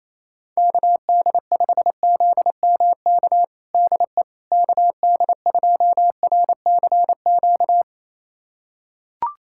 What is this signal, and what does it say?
Morse code, heard as a keyed tone, about 28 wpm, 700 Hz. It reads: KD5ZMK DE KD2RCQ
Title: USA Callsign DE Callsign - 6 Characters - 28wpm